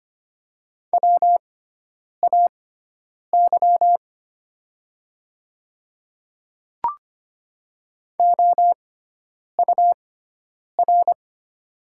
Morse code, heard as a keyed tone, about 25 wpm, spaced as for 9 wpm, 700 Hz. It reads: WAY OUR
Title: 6x Character Spacing - Top 200 Words - 25wpm